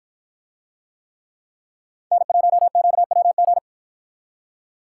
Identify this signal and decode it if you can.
Morse code, heard as a keyed tone, about 53 wpm, 700 Hz. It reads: N1QWG